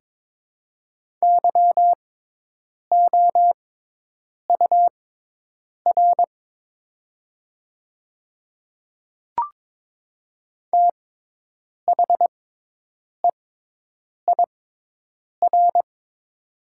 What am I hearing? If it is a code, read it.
Morse code, heard as a keyed tone, about 22 wpm, spaced as for 8 wpm, 700 Hz. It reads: YOUR THEIR
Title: 6x Character Spacing - Top 100 Words - 22wpm